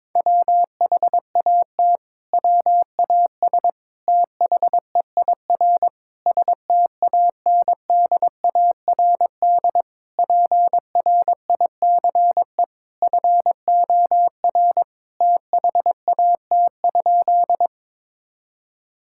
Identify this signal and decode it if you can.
Morse code, heard as a keyed tone, about 22 wpm, 700 Hz. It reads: WHAT WAS THEIR STANDARD PRICE FOR THAT?
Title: Sentences from Top 1000 Words - Review Part 6 - 22wpm